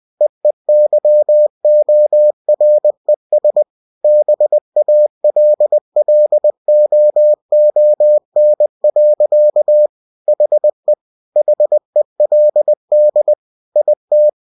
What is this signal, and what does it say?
Morse code, heard as a keyed tone, about 20 wpm, 600 Hz. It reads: EEYORES BALLOON. HE HELD IT